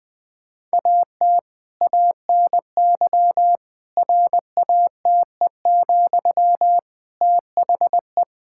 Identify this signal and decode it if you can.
Morse code, heard as a keyed tone, about 20 wpm, 700 Hz. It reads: AT ANY RATE, THE
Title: Sentences from Top 1000 Words - Review Part 6 - 20wpm